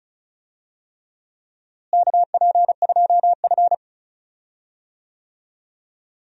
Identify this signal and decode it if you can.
Morse code, heard as a keyed tone, about 36 wpm, 700 Hz. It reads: KP2F
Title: DXCC-100 Call Signs - 4 Characters - Repeated 3x - 35wpm